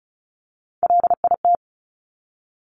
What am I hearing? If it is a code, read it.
Morse code, heard as a keyed tone, about 36 wpm, 700 Hz. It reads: LIT
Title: Top Three Letter Combinations in Words 35wpm